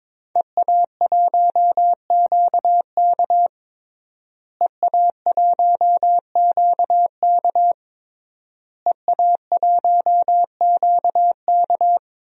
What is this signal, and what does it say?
Morse code, heard as a keyed tone, about 22 wpm, 700 Hz. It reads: EA1QK EA1QK EA1QK
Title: DXCC-100 Call Signs - 5 Characters - Repeated 3x - 22wpm